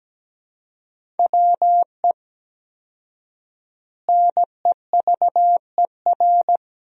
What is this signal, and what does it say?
Morse code, heard as a keyed tone, about 17 wpm, 700 Hz. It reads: WE NEVER
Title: 4x Word-Spacing - Sentences from Top 300 Words - 17wpm